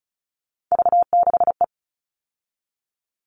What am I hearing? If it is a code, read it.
Morse code, heard as a keyed tone, about 34 wpm, 700 Hz. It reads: V6E
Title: All Cut Numbers - 3-Digit 35wpm